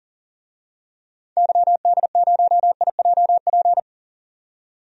Morse code, heard as a keyed tone, about 39 wpm, 700 Hz. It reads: YD0IJP
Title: DXCC-100 Call Signs - 6 Characters - 40wpm